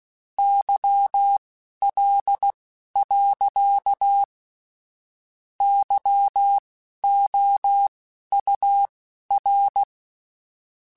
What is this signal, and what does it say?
Morse code, heard as a keyed tone, about 16 wpm, spaced as for 11 wpm, 800 Hz. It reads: YL. YOUR
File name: (spaced) / QSOs Spaced-mp3s / WA7QOX 2 @15 sp